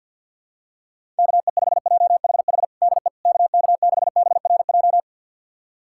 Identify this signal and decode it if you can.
Morse code, heard as a keyed tone, about 52 wpm, 700 Hz. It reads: KE5JHH DE KK6BRJ
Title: USA Callsign DE Callsign - 6 Characters - 50wpm